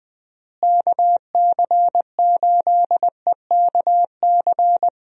Morse code, heard as a keyed tone, about 20 wpm, 700 Hz. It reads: KC8EKC
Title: USA Call Signs - 6 Characters - Repeated 2x - 20wpm